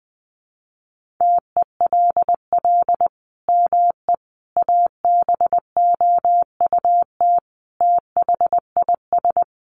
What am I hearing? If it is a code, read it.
Morse code, heard as a keyed tone, about 20 wpm, 700 Hz. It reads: TELL ME ABOUT THIS
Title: Sentences from Top 1400 Words 20wpm